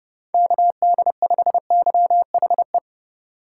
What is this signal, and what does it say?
Morse code, heard as a keyed tone, about 30 wpm, 700 Hz. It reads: KD5YHE
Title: Speed Racing - USA Call Signs 25wpm